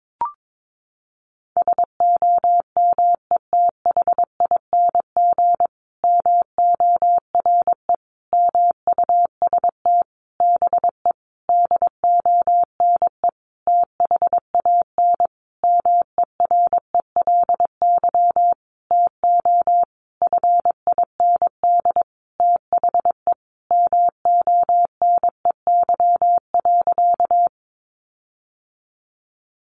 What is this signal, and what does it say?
Morse code, heard as a keyed tone, about 22 wpm, 700 Hz. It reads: SOMETHING MORE MUST BE DONE THAN MERELY TO FIND THE MONEY.
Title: Sentences from Top 1400 Words 22wpm